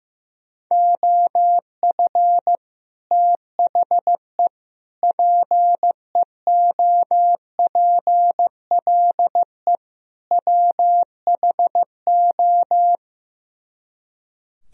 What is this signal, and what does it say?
Morse code, heard as a keyed tone, about 16 wpm, 700 Hz. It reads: OF THE PEOPLE WHO
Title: Speed-Racing - Sets of 4 Words 15wpm